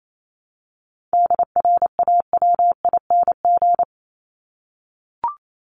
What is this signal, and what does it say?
Morse code, heard as a keyed tone, about 28 wpm, 700 Hz. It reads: DRAWING
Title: ING Suffix - Top 100 Words with Base Words 28wpm